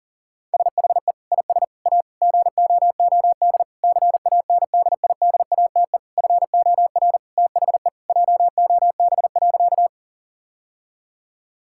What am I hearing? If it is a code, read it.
Morse code, heard as a keyed tone, about 40 wpm, 700 Hz. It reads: SHE IS A GOOD CANDIDATE FOR THE JOB.
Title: Sentences from Top 1600 Words 40wpm